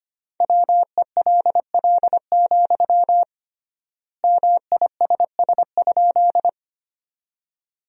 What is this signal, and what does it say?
Morse code, heard as a keyed tone, about 25 wpm, 700 Hz. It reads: WELL, MISS?
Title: 3x Word-Spacing - Sentences from Top 500 Words - 25wpm